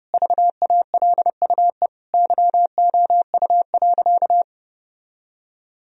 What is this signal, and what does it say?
Morse code, heard as a keyed tone, about 30 wpm, 700 Hz. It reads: VALUE YOU.
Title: Sentences from Top 1000 Words - Review Part 3 - 30wpm